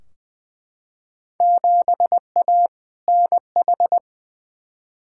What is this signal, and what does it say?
Morse code, heard as a keyed tone, about 21 wpm, 700 Hz. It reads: 7A NH
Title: Field Day Exchanges - 20wpm